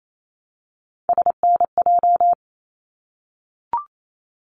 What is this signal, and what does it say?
Morse code, heard as a keyed tone, about 28 wpm, 700 Hz. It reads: SNJ
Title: ARRL Sections 28wpm